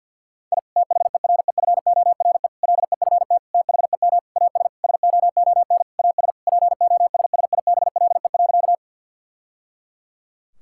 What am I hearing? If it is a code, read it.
Morse code, heard as a keyed tone, about 50 wpm, 700 Hz. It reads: I THEREFORE LEFT THEM AS SOON AS POSSIBLE.
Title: Sentences from Top 1000 Words - Review Part 5 - 50wpm